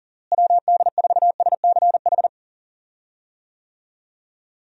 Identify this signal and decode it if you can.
Morse code, heard as a keyed tone, about 37 wpm, 700 Hz. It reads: WD4SCH
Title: USA Call Signs - 6 Characters - Repeated 2x - 40wpm